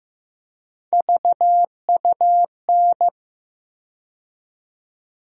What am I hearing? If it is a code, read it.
Morse code, heard as a keyed tone, about 15 wpm, 700 Hz. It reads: VUN
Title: USA Call Sign Postfixes - 3 Letters 15wpm